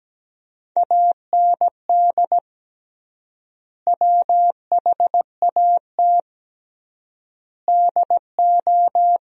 Morse code, heard as a keyed tone, about 17 wpm, 700 Hz. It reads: AND WHAT DO
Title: 3x Word-Spacing - Sentences from Top 200 Words - 17wpm